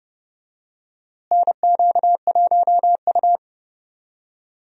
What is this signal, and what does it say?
Morse code, heard as a keyed tone, about 32 wpm, 700 Hz. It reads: NQ1U
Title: USA Call Signs - 4 Characters - Repeated 2x - 30wpm